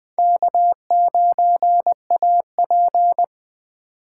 Morse code, heard as a keyed tone, about 20 wpm, 700 Hz. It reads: K9AP
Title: Speed Racing - USA Call Signs 17wpm